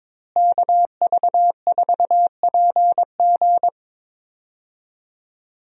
Morse code, heard as a keyed tone, about 22 wpm, 700 Hz. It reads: KV4PG
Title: USA Call Signs - 5 Characters - Repeated 3x - 22wpm